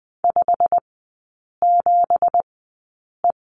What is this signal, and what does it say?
Morse code, heard as a keyed tone, about 20 wpm, spaced as for 8 wpm, 700 Hz. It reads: 57E
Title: 212 Lesson 34e - Number 8 - Three Characters 20wpm